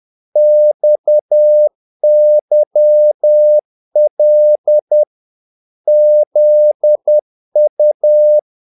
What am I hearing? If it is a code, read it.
Morse code, heard as a keyed tone, about 10 wpm, 600 Hz. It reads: XYL ZU
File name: dk5ke / Humor-50BpM